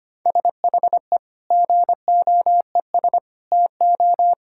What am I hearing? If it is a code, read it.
Morse code, heard as a keyed tone, about 25 wpm, 700 Hz. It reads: SHE GOES TO
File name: practice / 25wpm / Difficult Sentences with Hints - 25wpm